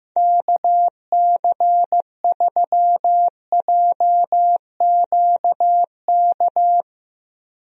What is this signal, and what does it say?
Morse code, heard as a keyed tone, about 15 wpm, 700 Hz. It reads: KC3JQK
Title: USA Call Signs - 6 Characters - Repeated 3x - 15wpm